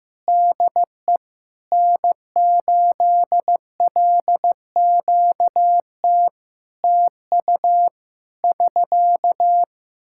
Morse code, heard as a keyed tone, about 15 wpm, 700 Hz. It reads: DE N8LQT TU <SK>
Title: Practice QSO - Exchange 4 - Ending - 15wpm